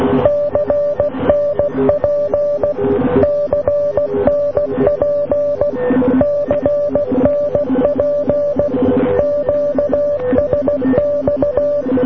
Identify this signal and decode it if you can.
Morse code, heard as a keyed tone, about 16 wpm, 580 Hz, with noise. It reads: CNP CNP CNP QSX